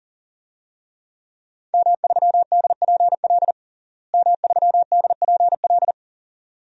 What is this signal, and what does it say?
Morse code, heard as a keyed tone, about 41 wpm, 700 Hz. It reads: M3DPL M3DPL
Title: DXCC-100 Call Signs - 5 Characters - Repeated 2x - 40wpm